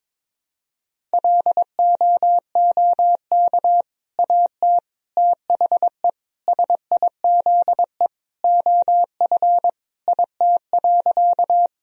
Morse code, heard as a keyed tone, about 22 wpm, 700 Hz. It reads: LOOK AT THE SIZE OF IT.
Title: Sentences from Top 1000 Words - Review Part 2 - 22wpm